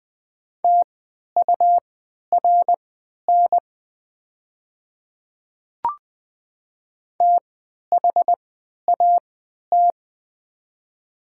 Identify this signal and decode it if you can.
Morse code, heard as a keyed tone, about 20 wpm, spaced as for 11 wpm, 700 Hz. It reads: TURN THAT
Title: 3x Character Spacing - Top 500 Words - 20wpm